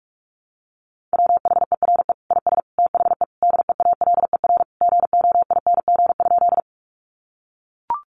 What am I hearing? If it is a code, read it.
Morse code, heard as a keyed tone, about 44 wpm, 700 Hz. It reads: WHERE IS THE DEALER GOING?
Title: Sentences from Top 1100 Words 45wpm